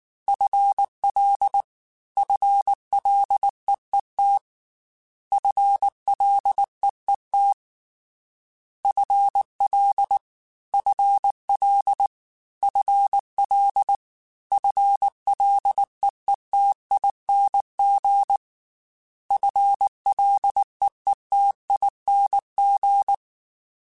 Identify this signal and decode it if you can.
Morse code, heard as a keyed tone, about 19 wpm, 800 Hz. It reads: FL FLEET FLEET FL FL FL FLEETING FLEETING
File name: FL / FL mp3s / FL Prefixes 3x2 Twice @18